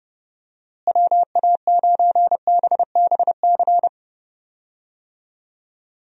Morse code, heard as a keyed tone, about 32 wpm, 700 Hz. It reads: WA9BBC
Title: USA Call Signs - 6 Characters - Repeated 2x - 30wpm